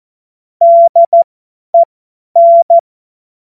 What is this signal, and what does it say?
Morse code, heard as a keyed tone, about 14 wpm, spaced as for 10 wpm, 680 Hz. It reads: DEN
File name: Morsekurs / Witze / Witz 17.Oktober 2023 50 F70